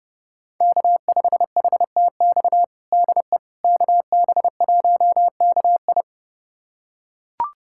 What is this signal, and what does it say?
Morse code, heard as a keyed tone, about 30 wpm, 700 Hz. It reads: K5HTX DE KB1KI
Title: USA Callsign DE Callsign - 5 Characters - 30wpm